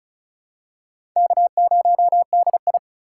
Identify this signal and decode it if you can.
Morse code, heard as a keyed tone, about 34 wpm, 700 Hz. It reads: K0DI
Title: USA Call Signs - 4 Characters - 4x Character Spacing - 35wpm